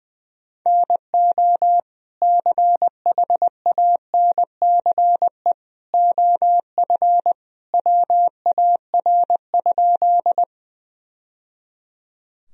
Morse code, heard as a keyed tone, about 20 wpm, 700 Hz. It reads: NO CHANCE OF WAR?
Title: Sentences from Top 1000 Words - Review Part 2 - 20wpm